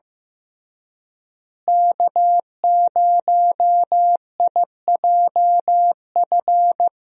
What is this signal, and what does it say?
Morse code, heard as a keyed tone, about 15 wpm, 700 Hz. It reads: K0IJF
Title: USA Call Signs - 5 Characters - Repeated 3x - 15wpm